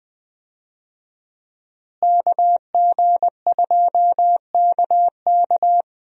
Morse code, heard as a keyed tone, about 20 wpm, 700 Hz. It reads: KG2KK
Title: USA Call Signs - 5 Characters - Repeated 3x - 20wpm